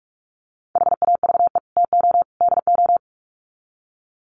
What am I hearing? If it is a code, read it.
Morse code, heard as a keyed tone, about 44 wpm, 700 Hz. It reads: HAVE TO DO
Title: Sets of 3 Words 45wpm